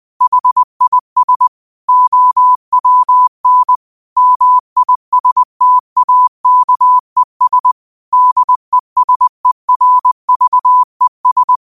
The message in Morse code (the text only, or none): HIS OWN MISTAKES DESERVES